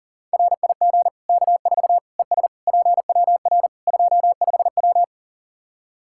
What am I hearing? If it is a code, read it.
Morse code, heard as a keyed tone, about 37 wpm, 700 Hz. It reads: RIG K4 ES PWR 25W
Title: Practice QSO - Exchange 2 - Info - 40wpm